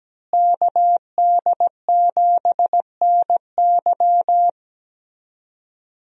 Morse code, heard as a keyed tone, about 17 wpm, 700 Hz. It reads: KD7NY
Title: USA Call Signs - 5 Characters - Repeated 3x - 17wpm